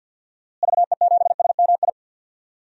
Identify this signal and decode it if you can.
Morse code, heard as a keyed tone, about 52 wpm, 700 Hz. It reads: VE7SMI
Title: DXCC-100 Call Signs - 6 Characters - Repeated 2x - 50wpm